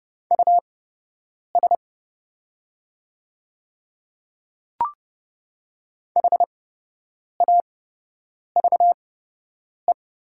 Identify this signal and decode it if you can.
Morse code, heard as a keyed tone, about 30 wpm, spaced as for 8 wpm, 700 Hz. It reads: US HAVE
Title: 8x Character Spacing - Top 500 Words - 30wpm